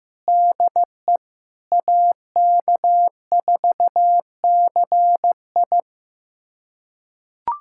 Morse code, heard as a keyed tone, about 16 wpm, 700 Hz. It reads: DE AK4CI
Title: USA Callsign DE Callsign - 5 Characters - 15wpm